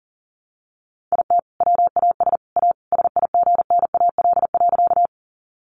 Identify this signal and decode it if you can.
Morse code, heard as a keyed tone, about 37 wpm, 700 Hz. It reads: IT WAS A SIGNAL.
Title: Sentences from Top 1400 Words 40wpm